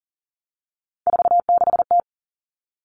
Morse code, heard as a keyed tone, about 40 wpm, 700 Hz. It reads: 46T
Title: All Cut Numbers - 3-Digit 40wpm